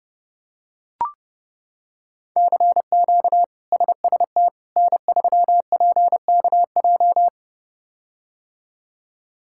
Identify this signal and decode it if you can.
Morse code, heard as a keyed tone, about 32 wpm, 700 Hz. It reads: CQ SST N3PKJ
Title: K1USN SST QSO Practice - 30wpm